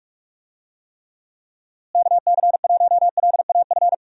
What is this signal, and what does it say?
Morse code, heard as a keyed tone, about 47 wpm, 700 Hz. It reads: KC1LAF
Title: USA Call Signs - 6 Characters - Repeated 3x - 45wpm